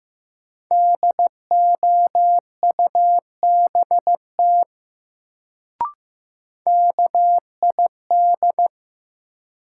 Morse code, heard as a keyed tone, about 15 wpm, 700 Hz. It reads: DOUBT KID
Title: Sets of 5 Words - New Word Drill - 15wpm